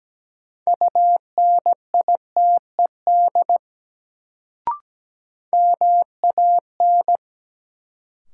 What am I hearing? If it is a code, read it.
Morse code, heard as a keyed tone, about 17 wpm, 700 Hz. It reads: UNITED MAN
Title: Sets of 2 Words - New Word Drill - 17wpm